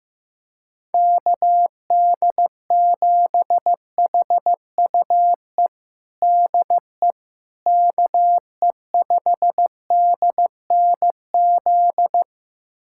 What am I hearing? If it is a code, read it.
Morse code, heard as a keyed tone, about 15 wpm, 700 Hz. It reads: KD7HUE DE KE5DNZ
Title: USA Callsign DE Callsign - 6 Characters - 15wpm